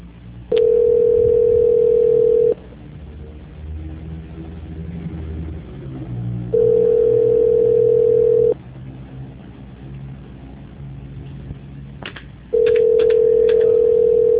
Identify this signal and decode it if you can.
Ringback tone (North America), 440 + 480 Hz.